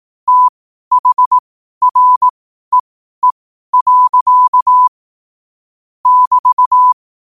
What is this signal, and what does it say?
Morse code, heard as a keyed tone, about 18 wpm, spaced as for 12 wpm, 1000 Hz. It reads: THREE. <BT>
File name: qotdfiles / 12wpm